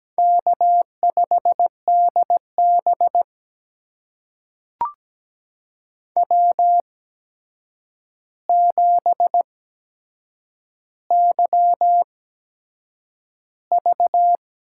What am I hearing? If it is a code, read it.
Morse code, heard as a keyed tone, about 17 wpm, spaced as for 7 wpm, 700 Hz. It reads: K5DB W7YV
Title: USA Call Signs - 8x Character Spacing - 17wpm